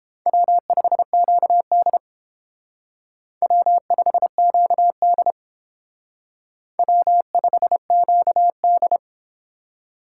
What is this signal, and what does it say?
Morse code, heard as a keyed tone, about 28 wpm, 700 Hz. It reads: W5QD W5QD W5QD